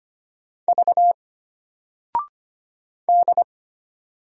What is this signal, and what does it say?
Morse code, heard as a keyed tone, about 25 wpm, 700 Hz. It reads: V D
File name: Single-Letter-Number-25wpm